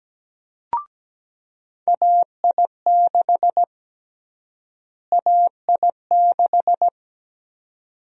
Morse code, heard as a keyed tone, about 17 wpm, 700 Hz. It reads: AI6 AI6
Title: USA Call Sign Prefixes - Repeated 2x - 17wpm